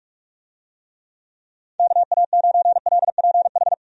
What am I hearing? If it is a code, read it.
Morse code, heard as a keyed tone, about 46 wpm, 700 Hz. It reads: KA9LPH
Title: USA Call Signs - 6 Characters - 45wpm